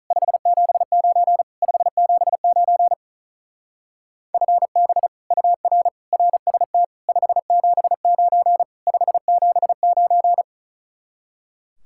Morse code, heard as a keyed tone, about 44 wpm, 700 Hz. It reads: 579 579 FB UR RST 579 579